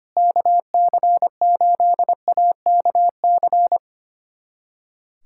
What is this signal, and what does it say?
Morse code, heard as a keyed tone, about 25 wpm, 700 Hz. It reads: KC8AKC
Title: USA Callsign DE Callsign - 6 Characters - 25wpm